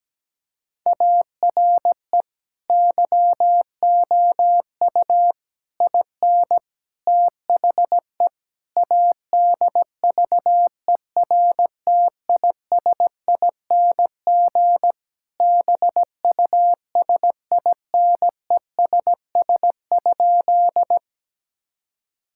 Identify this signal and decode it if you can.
Morse code, heard as a keyed tone, about 17 wpm, 700 Hz. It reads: ARE YOU IN THE ADVERTISING BUSINESS?
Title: Sentences from Top 1000 Words - Review Part 6 - 17wpm